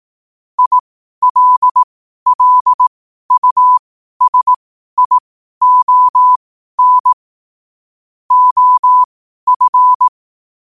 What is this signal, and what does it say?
Morse code, heard as a keyed tone, about 18 wpm, spaced as for 12 wpm, 1000 Hz. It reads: ILLUSION OF
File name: qotdfiles / 12wpm